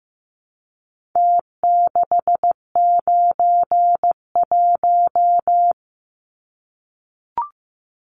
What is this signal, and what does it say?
Morse code, heard as a keyed tone, about 15 wpm, 700 Hz. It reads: T691
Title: Numbers - Contest Style - 4-Digit 15wpm